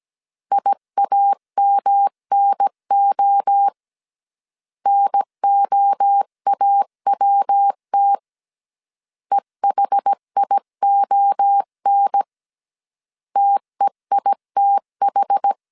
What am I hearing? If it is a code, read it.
Morse code, heard as a keyed tone, about 17 wpm, 800 Hz. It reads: IAMNO NOAWT EHION TEITH
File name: CW / DET / 1 3 1500C S A 16wpm 2141